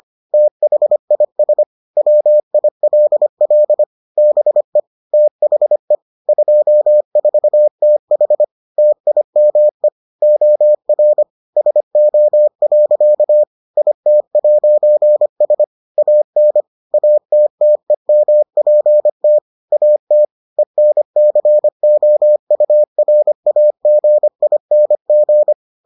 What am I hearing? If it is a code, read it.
Morse code, heard as a keyed tone, about 25 wpm, 600 Hz. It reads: THIS WILL BE THE 24TH TIME OR SO. IT'S AN ATTEMPT AT ENCOURAGING